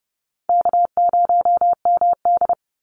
Morse code, heard as a keyed tone, about 30 wpm, 700 Hz. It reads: K0MD
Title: Easy Call Signs Repeated 3x - 30wpm